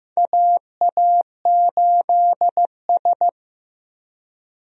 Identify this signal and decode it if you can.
Morse code, heard as a keyed tone, about 16 wpm, 700 Hz. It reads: AA8S
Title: USA Call Signs - 4 Characters - Repeated 3x - 15wpm